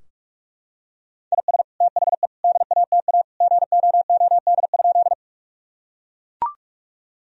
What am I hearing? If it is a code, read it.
Morse code, heard as a keyed tone, about 41 wpm, 700 Hz. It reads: IS THE DATA GOOD?
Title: Sentences from Top 1000 Words - Review Part 3 - 45wpm